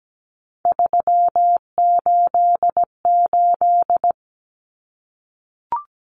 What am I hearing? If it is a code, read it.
Morse code, heard as a keyed tone, about 17 wpm, 700 Hz. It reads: 388
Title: Common Cut Numbers - 3-Digit 17wpm